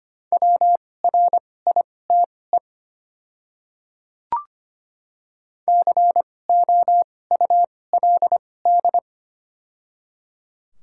Morse code, heard as a keyed tone, about 25 wpm, spaced as for 18 wpm, 700 Hz. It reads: WRITE COULD
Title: 2x Character Spacing - Top 500 Words - 25wpm